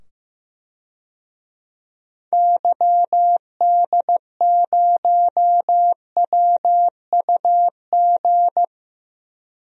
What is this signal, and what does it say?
Morse code, heard as a keyed tone, about 15 wpm, 700 Hz. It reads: YD0WUG
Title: DXCC-100 Call Signs - 6 Characters - 15wpm